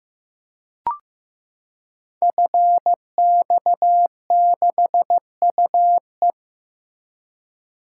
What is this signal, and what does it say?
Morse code, heard as a keyed tone, about 15 wpm, 700 Hz. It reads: FX6UE
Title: Code Groups - 5 Letter:Numbers - 15wpm